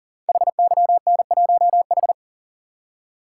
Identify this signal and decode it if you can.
Morse code, heard as a keyed tone, about 38 wpm, 690 Hz. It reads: HYN1H
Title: Code Groups - 5 Letter:Numbers - 40wpm